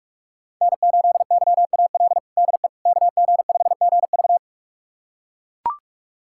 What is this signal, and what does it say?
Morse code, heard as a keyed tone, about 41 wpm, 700 Hz. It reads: N8YAL DE KG5GV
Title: USA Callsign DE Callsign - 5 Characters - 45wpm